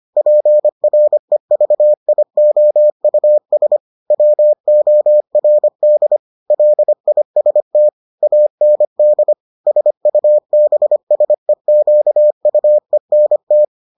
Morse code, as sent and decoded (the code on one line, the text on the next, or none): .--. .-. ...- .. --- ..- ... / .-- --- .-. -.. / .-.. .. ... - / .- -. -.. / ... ..- -... ... --.- ..- . -. -
PREVIOUS WORD LIST AND SUBSEQUENT